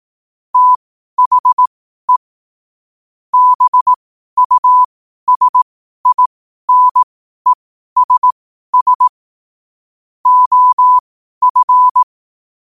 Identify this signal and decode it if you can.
Morse code, heard as a keyed tone, about 18 wpm, spaced as for 12 wpm, 1000 Hz. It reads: THE BUSINESS OF
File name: qotdfiles / 12wpm